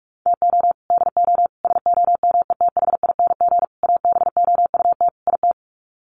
Morse code, heard as a keyed tone, about 44 wpm, 700 Hz. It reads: TO DO SOMETHING ABOUT IT
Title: Sets of 5 Words 45wpm